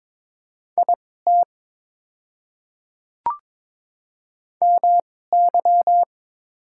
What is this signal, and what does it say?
Morse code, heard as a keyed tone, about 22 wpm, spaced as for 16 wpm, 700 Hz. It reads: IT MY
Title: 2x Character Spacing - Top 100 Words - 22wpm